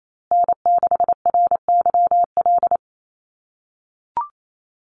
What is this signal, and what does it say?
Morse code, heard as a keyed tone, about 28 wpm, 700 Hz. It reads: N6RYL
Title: USA Call Signs Repeated 3x 28wpm